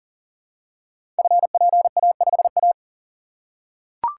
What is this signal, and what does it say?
Morse code, heard as a keyed tone, about 38 wpm, 690 Hz. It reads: FPA5A
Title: Code Groups - 5 Letter:Numbers - 40wpm